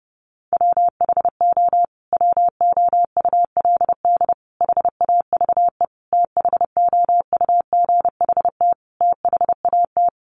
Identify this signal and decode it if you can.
Morse code, heard as a keyed tone, about 30 wpm, 700 Hz. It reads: WHO WOULD HAVE THOUGHT THAT
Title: Sentences from Top 1200 Words 30wpm